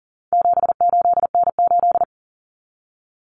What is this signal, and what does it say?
Morse code, heard as a keyed tone, about 40 wpm, 700 Hz. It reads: 78N8
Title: Common Cut Numbers - 4-Digit 40wpm